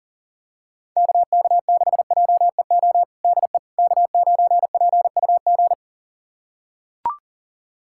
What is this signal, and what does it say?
Morse code, heard as a keyed tone, about 40 wpm, 700 Hz. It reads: KK6JEO DE K9PUG
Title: USA Callsign DE Callsign 40wpm